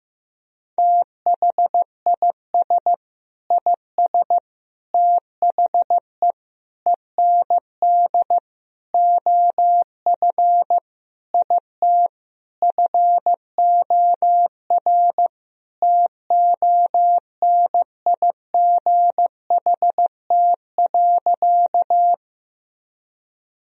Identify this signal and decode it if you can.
Morse code, heard as a keyed tone, about 16 wpm, 700 Hz. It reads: THIS IS THE END OF IT FOR TONIGHT.
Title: Sentences from Top 1000 Words - Review Part 3 - 15wpm